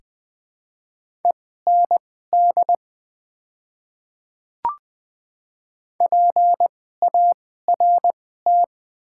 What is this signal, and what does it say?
Morse code, heard as a keyed tone, about 20 wpm, spaced as for 15 wpm, 700 Hz. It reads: END PART
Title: 2x Character Spacing - Top 200 Words - 20wpm